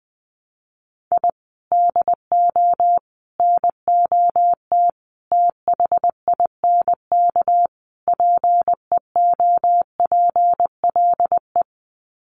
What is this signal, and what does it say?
Morse code, heard as a keyed tone, about 20 wpm, 700 Hz. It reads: I DO NOT THINK PEOPLE
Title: Sets of 5 Words Encore2 20wpm